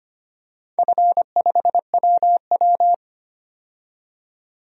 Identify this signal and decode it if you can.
Morse code, heard as a keyed tone, about 25 wpm, 700 Hz. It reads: F5WW
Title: DXCC-100 Call Signs - 4 Characters - 25wpm